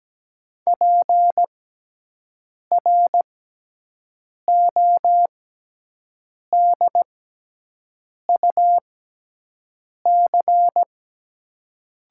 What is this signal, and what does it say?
Morse code, heard as a keyed tone, about 17 wpm, spaced as for 6 wpm, 700 Hz. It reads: PRODUC
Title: 6x Character Spacing - Top 500 Words - 17wpm